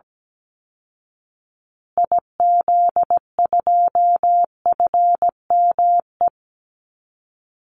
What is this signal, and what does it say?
Morse code, heard as a keyed tone, about 17 wpm, 700 Hz. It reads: IZ2FME
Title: Call Signs Repeated 2x - 17wpm